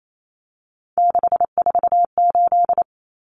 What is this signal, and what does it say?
Morse code, heard as a keyed tone, about 28 wpm, 700 Hz. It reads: 648
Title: Numbers - 3-Digit 28wpm